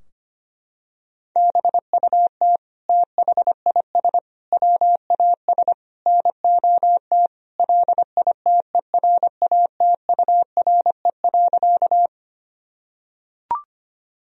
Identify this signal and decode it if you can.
Morse code, heard as a keyed tone, about 25 wpm, 700 Hz. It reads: BUT THIS WAS NOT LITERATURE.
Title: Sentences from Top 1000 Words - Review Part 3 - 25wpm